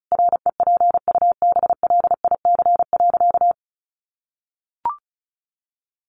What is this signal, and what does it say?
Morse code, heard as a keyed tone, about 34 wpm, 700 Hz. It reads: REPUBLIC.
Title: Sentences from Top 1500 Words 35wpm